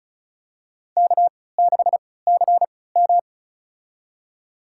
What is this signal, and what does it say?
Morse code, heard as a keyed tone, about 34 wpm, spaced as for 20 wpm, 700 Hz. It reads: K6CM
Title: USA Call Signs - 4 Characters - 3x Character Spacing - 35wpm